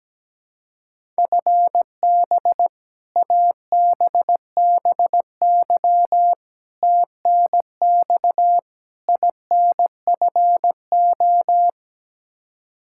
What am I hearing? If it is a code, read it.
Morse code, heard as a keyed tone, about 17 wpm, 700 Hz. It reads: FB ABBY TNX INFO
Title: Practice QSO - Exchange 3 - More Info - 17wpm